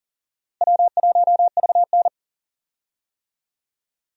Morse code, heard as a keyed tone, about 38 wpm, 700 Hz. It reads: W1VN
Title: USA Call Signs - 4 Characters - Repeated 2x - 40wpm